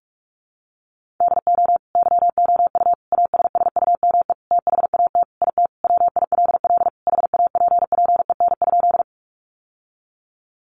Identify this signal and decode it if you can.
Morse code, heard as a keyed tone, about 44 wpm, 700 Hz. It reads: DO YOU ASSUME THAT IT WILL HAPPEN?
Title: Sentences from Top 1500 Words 45wpm